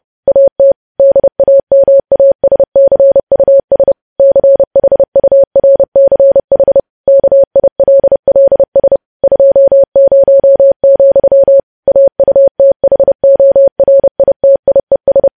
Morse code, heard as a keyed tone, about 30 wpm, 550 Hz. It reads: AT DAMASCUS CHURCH KILLS 20, AUTHORITIES
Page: news headlines in morse code at 30 wpm